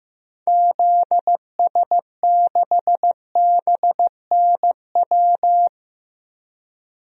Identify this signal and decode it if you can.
Morse code, heard as a keyed tone, about 15 wpm, 700 Hz. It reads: ZS6BNW
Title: DXCC-100 Call Signs - 6 Characters - 15wpm